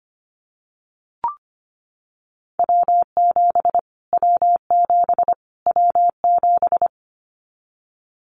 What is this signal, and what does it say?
Morse code, heard as a keyed tone, about 25 wpm, 700 Hz. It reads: W7 W7 W7
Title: Callsign Prefixes for Top 25 DXCC - Repeated 3x - 25wpm